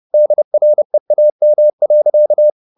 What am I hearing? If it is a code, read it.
Morse code, heard as a keyed tone, about 30 wpm, 600 Hz. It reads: DREAM.